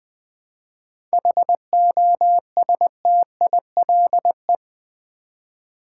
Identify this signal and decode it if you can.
Morse code, heard as a keyed tone, about 20 wpm, 700 Hz. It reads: HOSTILE
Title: 7 Letter Words - First 3 Letters Given - 20wpm